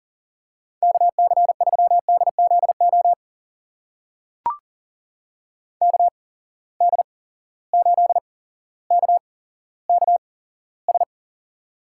Morse code, heard as a keyed tone, about 38 wpm, spaced as for 17 wpm, 700 Hz. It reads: KC3DZO KD8KKS
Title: USA Call Signs - 6 Characters - 8x Character Spacing - 40wpm